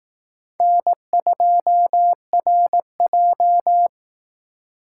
Morse code, heard as a keyed tone, about 18 wpm, 700 Hz. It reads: N2RJ
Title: Speed Racing - USA 1x2 Call Signs 15wpm